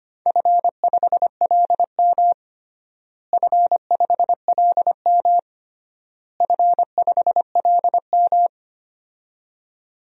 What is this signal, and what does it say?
Morse code, heard as a keyed tone, about 25 wpm, 700 Hz. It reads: F5LM F5LM F5LM
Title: DXCC-100 Call Signs - 4 Characters - Repeated 3x - 25wpm